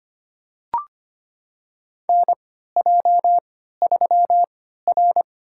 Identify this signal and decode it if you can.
Morse code, heard as a keyed tone, about 25 wpm, spaced as for 14 wpm, 700 Hz. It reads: NJ3R
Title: USA Call Signs - 4 Characters - 3x Character Spacing - 25wpm